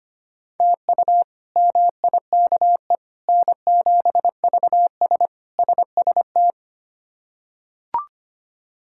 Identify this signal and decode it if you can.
Morse code, heard as a keyed tone, about 25 wpm, 700 Hz. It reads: TU MIKE N7VS SST
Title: K1USN SST QSO Practice - 25wpm